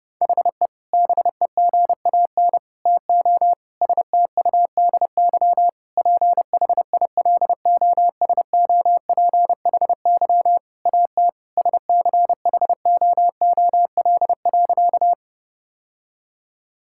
Morse code, heard as a keyed tone, about 30 wpm, 700 Hz. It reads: HE BEGAN TO STUDY PHILOSOPHY AT SCHOOL.
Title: Sentences from Top 1000 Words - Review Part 6 - 30wpm